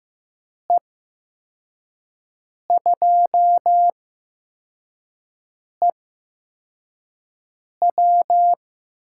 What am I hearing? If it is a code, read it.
Morse code, heard as a keyed tone, about 15 wpm, spaced as for 4 wpm, 700 Hz. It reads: E2EW
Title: USA Call Signs - 5 Characters - 8x Character Spacing - 15wpm